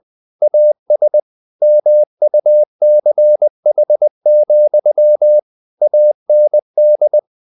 Morse code, heard as a keyed tone, about 20 wpm, 600 Hz. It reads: AS MUCH, AND
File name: the old man and the sea / the old man and the sea part 1 20wpm